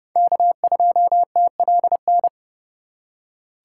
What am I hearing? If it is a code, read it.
Morse code, heard as a keyed tone, about 30 wpm, 700 Hz. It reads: K2TLN
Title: USA Call Signs - 2x Character Spacing - 30wpm